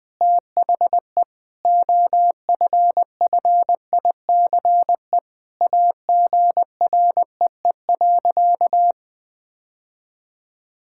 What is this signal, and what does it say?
Morse code, heard as a keyed tone, about 20 wpm, 700 Hz. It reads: THE OFFICE AGREE.
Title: Sentences with 5-Word Phrases - 20wpm